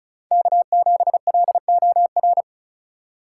Morse code, heard as a keyed tone, about 36 wpm, 700 Hz. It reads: K7LOR
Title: USA Call Signs - 2x Character Spacing - 35wpm